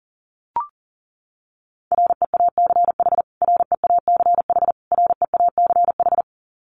Morse code, heard as a keyed tone, about 40 wpm, 700 Hz. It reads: REACH REACH REACH
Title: Top 500 Words Repeated 40wpm